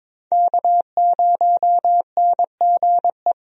Morse code, heard as a keyed tone, about 22 wpm, 700 Hz. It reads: K0NGE
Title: USA Call Signs - 5 Characters - Repeated 2x - 22wpm